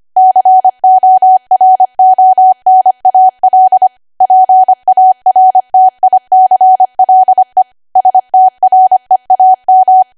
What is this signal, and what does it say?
Morse code, heard as a keyed tone, about 25 wpm, 750 Hz. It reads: CORONAL PARTICLE STREAM